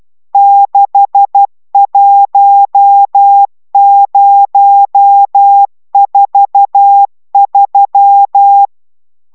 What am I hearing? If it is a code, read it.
Morse code, heard as a keyed tone, about 12 wpm, 800 Hz. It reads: 61043